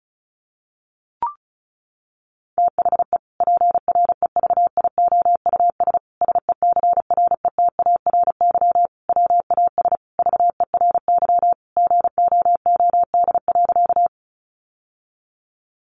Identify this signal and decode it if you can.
Morse code, heard as a keyed tone, about 34 wpm, 700 Hz. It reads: THE PREVIOUS SECRETARY WAS VERY GOOD.
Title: Sentences from Top 1500 Words 35wpm